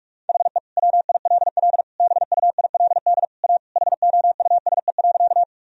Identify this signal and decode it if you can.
Morse code, heard as a keyed tone, about 46 wpm, 700 Hz. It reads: HE WILL BUILD A HOUSE.
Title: Sentences from Top 1000 Words - Review Part 1 - 45wpm